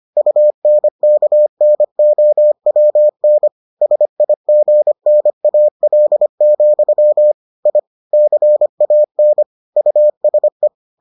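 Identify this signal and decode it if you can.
Morse code, heard as a keyed tone, about 25 wpm, 600 Hz. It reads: UNKNOWN SIGNAL, I CAN USE
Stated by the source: Morse edition of the Foundations of Amateur Radio episodes.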